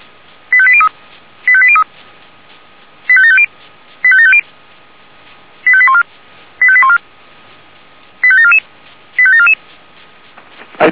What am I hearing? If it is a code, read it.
Five-tone selective call (ZVEI), heard as quick runs of five tones, tones of about 70 ms. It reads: 85803 85803 85750 85750 85814 85814 85744 85744
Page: /464/SDR/ZVEI/